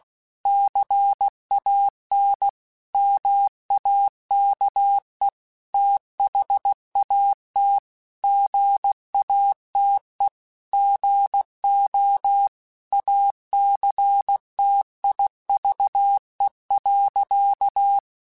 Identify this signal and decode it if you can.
Morse code, heard as a keyed tone, about 16 wpm, 800 Hz. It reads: CAN MAKE THAT GATE GO ACTIVE.
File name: Technical Subjects / Technical Subjects-mp3s / OR Gate @15